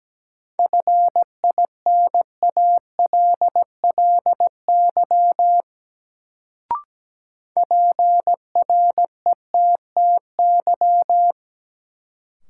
Morse code, heard as a keyed tone, about 17 wpm, 700 Hz. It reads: FINALLY PRETTY
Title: Sentences from Top 1000 Words - New Word Drill - 17wpm